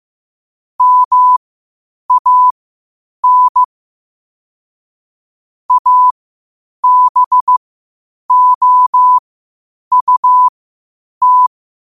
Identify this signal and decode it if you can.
Morse code, heard as a keyed tone, about 15 wpm, spaced as for 8 wpm, 1000 Hz. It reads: MAN ABOUT
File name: qotdfiles / 8wpm